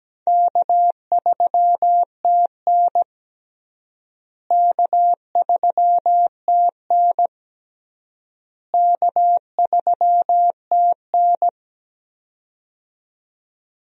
Morse code, heard as a keyed tone, about 17 wpm, 700 Hz. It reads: K3TN K3TN K3TN
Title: USA Call Signs - 4 Characters - Repeated 3x - 17wpm